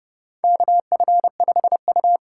Morse code, heard as a keyed tone, about 31 wpm, 700 Hz. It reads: KF5U
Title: USA Call Signs - 8x Character Spacing - 30wpm